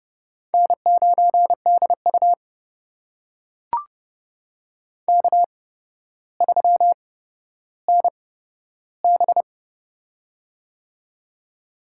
Morse code, heard as a keyed tone, about 30 wpm, spaced as for 13 wpm, 700 Hz. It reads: N9DU K3NB
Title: USA Call Signs - 4 Characters - 8x Character Spacing - 30wpm